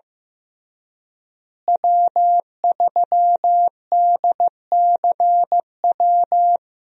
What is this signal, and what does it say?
Morse code, heard as a keyed tone, about 15 wpm, 700 Hz. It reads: W3DCW
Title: USA Call Signs - 5 Characters - Repeated 3x - 15wpm